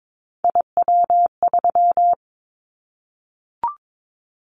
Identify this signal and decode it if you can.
Morse code, heard as a keyed tone, about 22 wpm, 700 Hz. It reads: IW3
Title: Callsign Prefixes for Top 25 DXCC - Repeated 3x - 22wpm